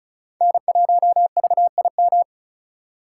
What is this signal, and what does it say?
Morse code, heard as a keyed tone, about 34 wpm, 700 Hz. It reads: N1VIM
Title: USA Call Signs - 5 Characters - 3x Character Spacing - 35wpm